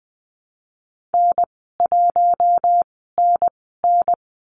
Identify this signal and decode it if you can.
Morse code, heard as a keyed tone, about 20 wpm, spaced as for 14 wpm, 700 Hz. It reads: N1NN
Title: 048 Lesson 9g - Number 4 - Call Signs 20wpm